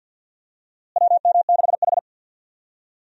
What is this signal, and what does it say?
Morse code, heard as a keyed tone, about 49 wpm, 700 Hz. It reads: WM6H
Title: USA Call Signs - 4 Characters - Repeated 2x - 50wpm